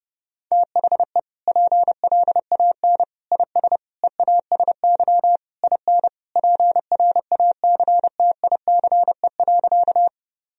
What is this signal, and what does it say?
Morse code, heard as a keyed tone, about 30 wpm, 700 Hz. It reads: THE PLAN IS EASY IN PRACTICE.
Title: Sentences from Top 1000 Words - Review Part 4 - 30wpm